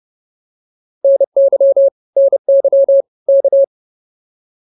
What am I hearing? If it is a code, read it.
Morse code, heard as a keyed tone, about 30 wpm, 550 Hz. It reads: NY NY K